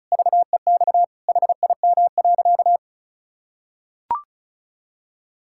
Morse code, heard as a keyed tone, about 36 wpm, 700 Hz. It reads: VEX HIM.
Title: Pangram Challenge v2 - 35wpm